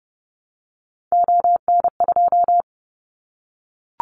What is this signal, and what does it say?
Morse code, heard as a keyed tone, about 30 wpm, 700 Hz. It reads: ON2
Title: Callsign Prefixes for Top 50 DXCC - 30wpm